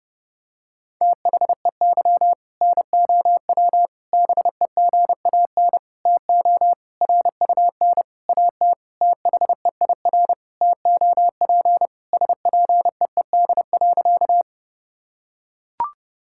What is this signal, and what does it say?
Morse code, heard as a keyed tone, about 32 wpm, 700 Hz. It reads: THEY NOW BEGAN TO RUN AT THEIR TOP SPEED.
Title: Sentences from Top 1000 Words - Review Part 5 - 30wpm